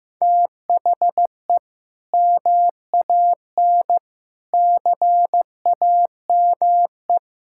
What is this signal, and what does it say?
Morse code, heard as a keyed tone, about 15 wpm, 700 Hz. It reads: THE MAN CAME
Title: Sentences from Top 1000 Words - Review Part 6 - 15wpm